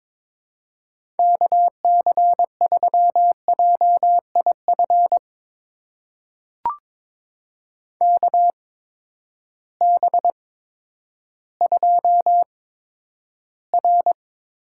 Morse code, heard as a keyed tone, about 22 wpm, spaced as for 11 wpm, 700 Hz. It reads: KC3JIF KB2R
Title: USA Call Signs - 6 Characters - 8x Character Spacing - 22wpm